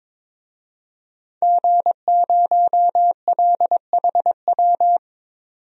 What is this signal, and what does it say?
Morse code, heard as a keyed tone, about 21 wpm, 700 Hz. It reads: G0LHW
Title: DXCC-100 Call Signs - 5 Characters - 22wpm